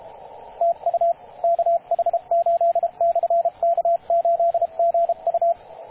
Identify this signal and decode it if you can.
Morse code, heard as a keyed tone, about 27 wpm, 680 Hz, with radio noise. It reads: TU KH8/K8GU
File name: posts / 2016-apr-29-0441z-kh8-k8gu-18-069mhz-cw